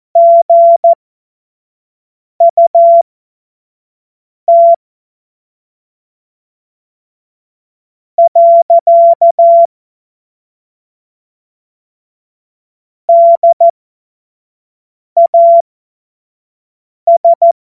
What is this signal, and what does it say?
Morse code, heard as a keyed tone, about 14 wpm, spaced as for 5 wpm, 680 Hz. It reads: GUT . DAS
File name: Morsekurs / Witze / Witz 19. Oktober 2021, 25 F70